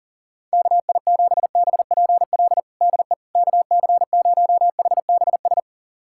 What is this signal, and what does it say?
Morse code, heard as a keyed tone, about 40 wpm, 700 Hz. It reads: KI7BPL DE KC0HBS
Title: USA Callsign DE Callsign - 6 Characters - 40wpm